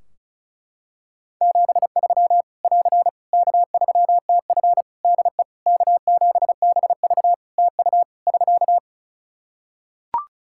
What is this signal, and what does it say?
Morse code, heard as a keyed tone, about 34 wpm, 700 Hz. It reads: 73 <AR> K3TF DE K7BV TU <SK>
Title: Practice QSO - Exchange 4 - Ending - 35wpm